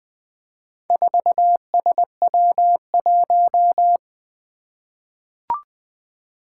Morse code, heard as a keyed tone, about 20 wpm, 700 Hz. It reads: 4SW1
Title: Four Challenging Letter-Number Combinations 20wpm